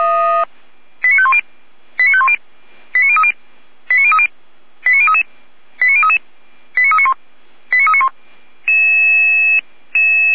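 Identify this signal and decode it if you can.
Five-tone selective call (ZVEI), heard as quick runs of five tones, tones of about 70 ms. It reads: 89510 89510 89030 89030 89033 89033 89391 89391
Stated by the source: SDR receiver, ZVEI five-tone selective call, address 89050.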